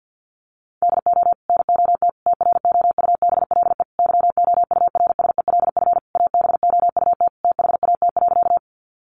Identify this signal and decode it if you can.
Morse code, heard as a keyed tone, about 50 wpm, 700 Hz. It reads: DO NOT TROUBLE YOURSELF ABOUT THAT.
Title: Sentences from Top 1200 Words 50wpm